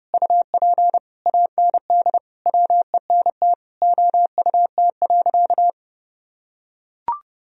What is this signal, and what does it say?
Morse code, heard as a keyed tone, about 32 wpm, 700 Hz. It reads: UP AND WENT OUT.
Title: Sentences from Top 1000 Words - Review Part 1 - 30wpm